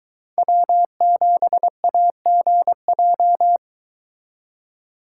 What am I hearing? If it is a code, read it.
Morse code, heard as a keyed tone, about 22 wpm, 700 Hz. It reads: W7AGJ